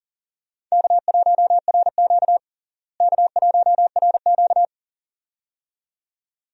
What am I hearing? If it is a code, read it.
Morse code, heard as a keyed tone, about 40 wpm, 700 Hz. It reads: K1RQ K1RQ
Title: USA Call Signs - 4 Characters - Repeated 2x - 40wpm